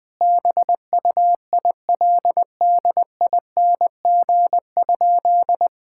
Morse code, heard as a keyed tone, about 20 wpm, 700 Hz. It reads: BUILDING?
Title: Sentences from Top 1000 Words - Review Part 6 - 20wpm